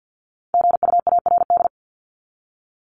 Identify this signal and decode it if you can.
Morse code, heard as a keyed tone, about 50 wpm, 700 Hz. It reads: GUARD
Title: Top 501-750 Words 50wpm